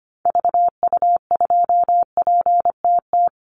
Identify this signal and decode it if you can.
Morse code, heard as a keyed tone, about 25 wpm, 700 Hz. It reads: VU2PTT
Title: Call Signs - 25wpm